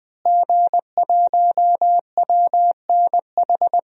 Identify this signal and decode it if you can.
Morse code, heard as a keyed tone, about 20 wpm, 700 Hz. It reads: G1WNH
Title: DXCC-100 Call Signs - 5 Characters - Repeated 3x - 20wpm